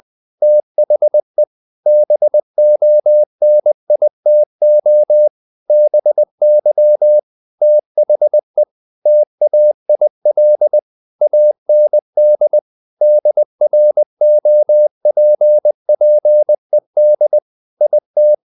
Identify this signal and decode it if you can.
Morse code, heard as a keyed tone, about 20 wpm, 600 Hz. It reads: THE BONITO BY THE TAIL AND DROPPED IT
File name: the old man and the sea / the old man and the sea part 1 20wpm